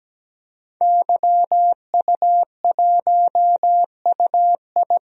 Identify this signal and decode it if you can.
Morse code, heard as a keyed tone, about 17 wpm, 700 Hz. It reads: YU1UI
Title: DXCC-100 Call Signs - 5 Characters - 17wpm